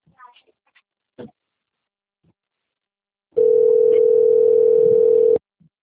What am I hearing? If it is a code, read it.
Ringback tone (North America), 440 + 480 Hz.